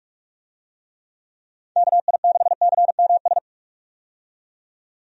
Morse code, heard as a keyed tone, about 44 wpm, 700 Hz. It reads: KI6CMS
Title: USA Call Signs - 6 Characters - 45wpm